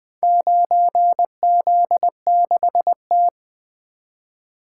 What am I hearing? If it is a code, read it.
Morse code, heard as a keyed tone, about 20 wpm, 700 Hz. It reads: 9Z6T
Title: Four Letter-Number Combinations - 20wpm